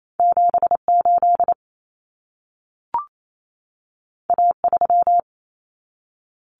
Morse code, heard as a keyed tone, about 28 wpm, 700 Hz. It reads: 78 A3
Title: Common Cut Numbers - 2-Digit 28wpm